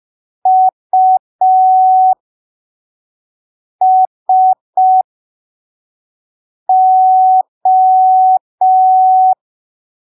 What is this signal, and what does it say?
Morse code, heard as a keyed tone, about 5 wpm, spaced as for 3 wpm, 750 Hz. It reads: USO
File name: arquivos / texto / texto 18 05ppm